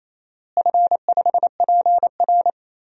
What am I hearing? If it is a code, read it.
Morse code, heard as a keyed tone, about 28 wpm, 700 Hz. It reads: F5PR
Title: DXCC-100 Call Signs - 4 Characters - Repeated 3x - 28wpm